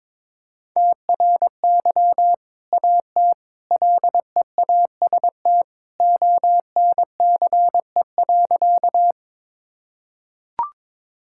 Morse code, heard as a keyed tone, about 22 wpm, 700 Hz. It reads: TRY AT LEAST ONCE.
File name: practice / 22wpm / Sentences with 2-Word Phrases - 22wpm